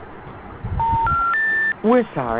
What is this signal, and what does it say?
Special information tone, with speech.
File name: recordings / RE849c1d9c88799f2a803e77f7f8edf7c2